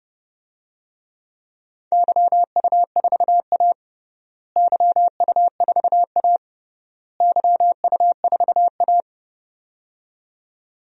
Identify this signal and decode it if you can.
Morse code, heard as a keyed tone, about 29 wpm, 700 Hz. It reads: YU4A YU4A YU4A
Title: DXCC-100 Call Signs - 4 Characters - Repeated 3x - 30wpm